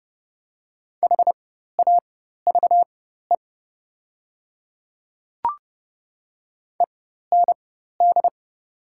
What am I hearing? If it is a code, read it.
Morse code, heard as a keyed tone, about 30 wpm, spaced as for 14 wpm, 700 Hz. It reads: HAVE END
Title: 4x Character Spacing - Top 200 Words - 30wpm